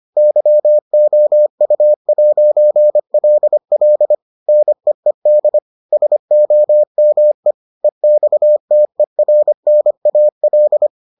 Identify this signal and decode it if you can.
Morse code, heard as a keyed tone, about 25 wpm, 600 Hz. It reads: YOU'LL NEED SOME EXTERNAL